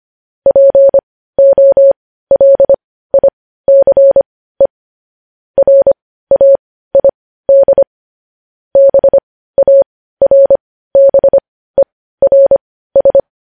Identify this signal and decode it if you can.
Morse code, heard as a keyed tone, about 25 wpm, spaced as for 15 wpm, 550 Hz. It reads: POLICE RAID BARBERS
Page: news-headlines-in-morse-code-at-15-wpm